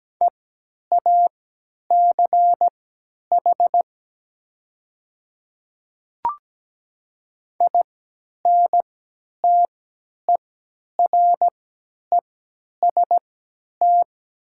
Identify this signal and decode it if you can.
Morse code, heard as a keyed tone, about 17 wpm, spaced as for 10 wpm, 690 Hz. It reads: EACH INTEREST
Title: 3x Character Spacing - Top 500 Words - 17wpm